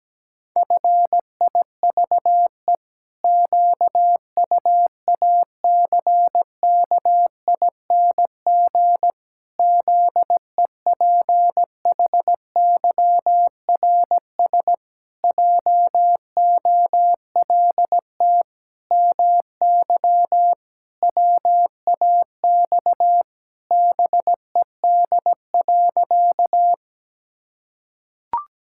Morse code, heard as a keyed tone, about 17 wpm, 700 Hz. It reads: FIVE QUACKING ZEPHYRS JOLT MY WAX BED.
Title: Pangram Challenge v2 - 17wpm